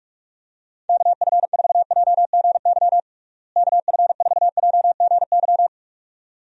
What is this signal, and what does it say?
Morse code, heard as a keyed tone, about 47 wpm, 700 Hz. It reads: KF4JGY KF4JGY
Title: USA Call Signs - 6 Characters - Repeated 2x - 45wpm